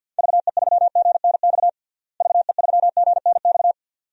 Morse code, heard as a keyed tone, about 50 wpm, 700 Hz. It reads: VE3GNX VE3GNX
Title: DXCC-100 Call Signs - 6 Characters - Repeated 2x - 50wpm